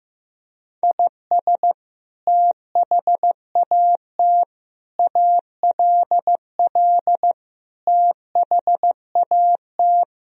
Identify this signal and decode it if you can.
Morse code, heard as a keyed tone, about 15 wpm, 700 Hz. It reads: IS THAT ALL THAT
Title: Sentences from Top 1000 Words - Review Part 3 - 15wpm